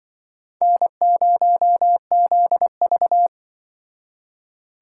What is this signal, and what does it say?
Morse code, heard as a keyed tone, about 24 wpm, 700 Hz. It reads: N0ZV
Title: Speed Racing - USA 1x2 Call Signs 20wpm